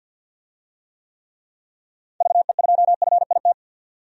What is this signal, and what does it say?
Morse code, heard as a keyed tone, about 53 wpm, 700 Hz. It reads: VE2FIT
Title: DXCC-100 Call Signs - 6 Characters - 50wpm